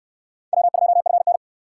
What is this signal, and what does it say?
Morse code, heard as a keyed tone, about 59 wpm, 700 Hz. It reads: W2FN